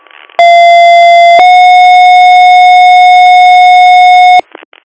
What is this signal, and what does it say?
Two-tone page, heard as a one-second tone, then a three-second tone, about 690 then 730 Hz, with noise. It is heard from 0.4 to 4.4 s.